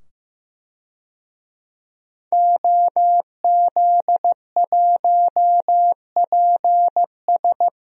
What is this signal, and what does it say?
Morse code, heard as a keyed tone, about 15 wpm, 700 Hz. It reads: OZ1PS
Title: DXCC-100 Call Signs - 5 Characters - 15wpm